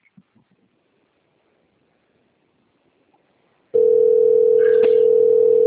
Ringback tone (North America), 440 + 480 Hz.